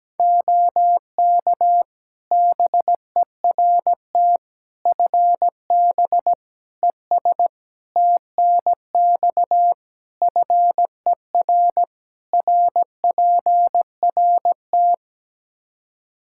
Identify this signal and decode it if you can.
Morse code, heard as a keyed tone, about 17 wpm, 700 Hz. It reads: OK BERT FB ES TNX FER RPRT
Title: Practice QSO - Exchange 2 - Info - 17wpm